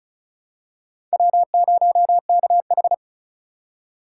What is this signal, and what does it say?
Morse code, heard as a keyed tone, about 36 wpm, 700 Hz. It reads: W0KH
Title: USA Call Signs - 4 Characters - 2x Character Spacing - 35wpm